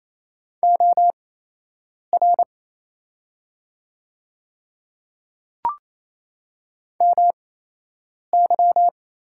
Morse code, heard as a keyed tone, about 28 wpm, spaced as for 8 wpm, 700 Hz. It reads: OR MY
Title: 8x Character Spacing - Top 100 Words - 28wpm